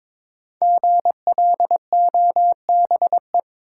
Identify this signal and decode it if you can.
Morse code, heard as a keyed tone, about 22 wpm, 700 Hz. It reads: GLOBE
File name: 5 Letter Words - First 3 Letters Given - 22wpm